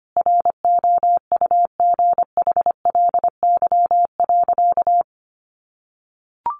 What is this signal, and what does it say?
Morse code, heard as a keyed tone, about 25 wpm, 700 Hz. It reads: ROUGHLY.
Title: Sentences from Top 1500 Words 25wpm